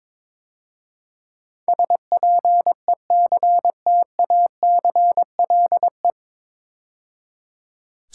Morse code, heard as a keyed tone, about 21 wpm, 700 Hz. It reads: SPECTACLE
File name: 9 Letter Words - First 3 Letters Given - 22wpm